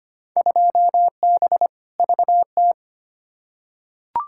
Morse code, heard as a keyed tone, about 25 wpm, 700 Hz. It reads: 2B VT
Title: Field Day Exchanges Encore - 25wpm